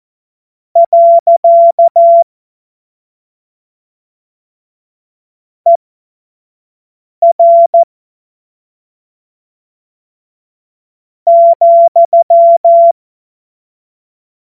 Morse code, heard as a keyed tone, about 14 wpm, spaced as for 3 wpm, 680 Hz. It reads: . ER,
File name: Morsekurs / Witze / Witz 12. April 2022 25 F70